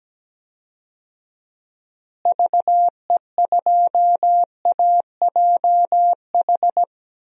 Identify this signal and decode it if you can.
Morse code, heard as a keyed tone, about 17 wpm, 700 Hz. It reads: VE2AJH
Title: DXCC-100 Call Signs - 6 Characters - Repeated 2x - 17wpm